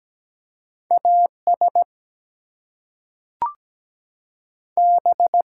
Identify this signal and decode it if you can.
Morse code, heard as a keyed tone, about 17 wpm, 700 Hz. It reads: AS B
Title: Two Letter Words - 17wpm